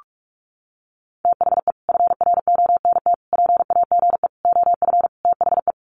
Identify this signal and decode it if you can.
Morse code, heard as a keyed tone, about 44 wpm, 700 Hz. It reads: THE FRONT PAGE OF THE
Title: Sets of 5 Words Encore 45wpm